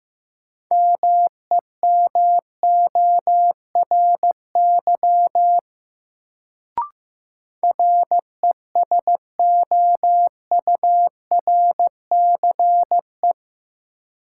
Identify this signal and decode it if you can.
Morse code, heard as a keyed tone, about 15 wpm, 700 Hz. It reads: MEMORY RESOURCE
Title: Sentences from Top 1300 Words - New Word Drill Alt - 15wpm